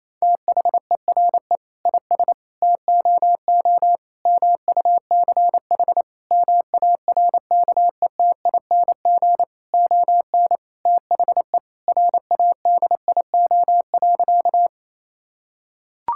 Morse code, heard as a keyed tone, about 28 wpm, 700 Hz. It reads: THERE IS TOO MUCH MARKETING ON THE RADIO.
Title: Sentences from Top 1000 Words - Review Part 5 - 28wpm